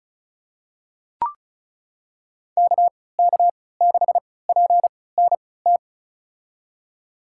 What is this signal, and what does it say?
Morse code, heard as a keyed tone, about 34 wpm, spaced as for 20 wpm, 690 Hz. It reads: KK6PNT
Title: USA Call Signs - 3x Character Spacing - 35wpm